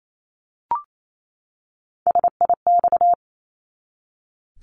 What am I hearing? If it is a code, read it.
Morse code, heard as a keyed tone, about 28 wpm, 700 Hz. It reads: SIX
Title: Top 500 Words 28wpm